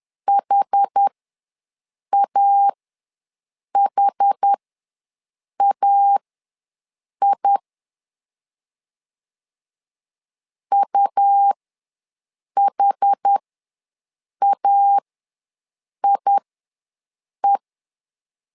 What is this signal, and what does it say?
Morse code, heard as a keyed tone, about 10 wpm, spaced as for 6 wpm, 800 Hz. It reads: HAHAI UHAIE
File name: LEZIONI / 2 UHDN / 2 UH ws 10wpm 2029